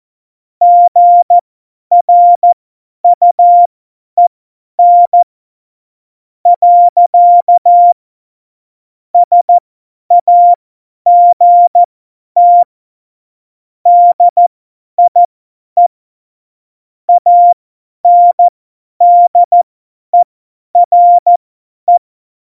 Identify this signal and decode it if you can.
Morse code, heard as a keyed tone, about 14 wpm, spaced as for 10 wpm, 700 Hz. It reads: GRUEN. SAGT DIE ANDERE
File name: Morsekurs / Witze / Witz 04. September 2018, 50 F70